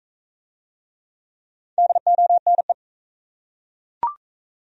Morse code, heard as a keyed tone, about 40 wpm, 700 Hz. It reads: DONE